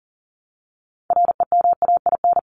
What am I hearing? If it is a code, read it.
Morse code, heard as a keyed tone, about 40 wpm, 700 Hz. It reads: REMAIN
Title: ING Suffix - Top 100 Words with Base Words 40wpm